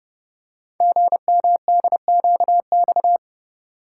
Morse code, heard as a keyed tone, about 30 wpm, 700 Hz. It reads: GMDQX
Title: Code Groups - 5 Letters - 30wpm